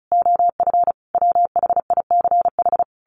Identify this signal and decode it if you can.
Morse code, heard as a keyed tone, about 34 wpm, 700 Hz. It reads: OF WHICH
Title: A Sound Of Thunder 35wpm